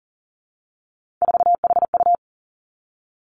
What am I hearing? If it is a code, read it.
Morse code, heard as a keyed tone, about 41 wpm, 700 Hz. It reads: 4HU
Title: Three Challenging Letter-Number Combinations 40wpm